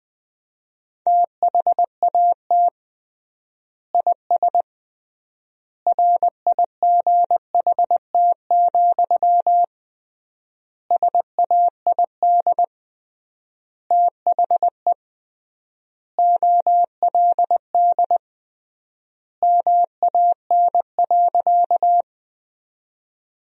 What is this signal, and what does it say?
Morse code, heard as a keyed tone, about 20 wpm, 700 Hz. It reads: THAT IS RIGHT, SAID THE OLD MAN.
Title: 3x Word-Spacing - Sentences from Top 200 Words - 20wpm